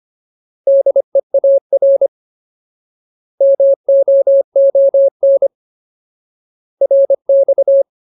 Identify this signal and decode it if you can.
Morse code, heard as a keyed tone, about 25 wpm, 550 Hz. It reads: DEAR MOON RX
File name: Ultimate Word List / 25WPM Ultimate